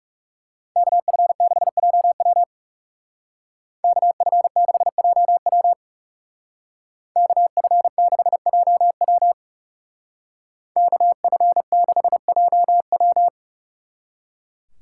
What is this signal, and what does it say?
Morse code, heard as a keyed tone, about 36 wpm, 700 Hz. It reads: KF6JW KF6JW KF6JW KF6JW
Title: Speed Racing - USA 2x2 Call Signs 30wpm